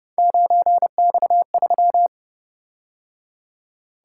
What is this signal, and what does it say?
Morse code, heard as a keyed tone, about 29 wpm, 700 Hz. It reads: 9X3